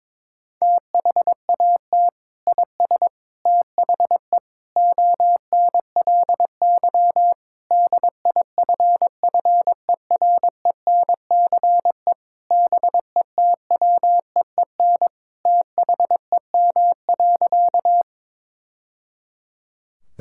Morse code, heard as a keyed tone, about 22 wpm, 700 Hz. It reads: THAT IS THE ONLY DIFFERENCE BETWEEN THEM.
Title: Sentences from Top 1000 Words - Review Part 5 - 22wpm